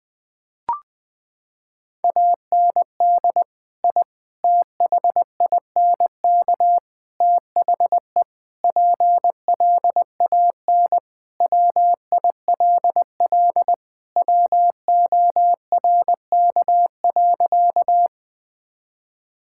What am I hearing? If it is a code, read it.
Morse code, heard as a keyed tone, about 20 wpm, 700 Hz. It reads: AND I THINK THE PLAN WILL WORK.
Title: Sentences with 4-Word Phrases - 20wpm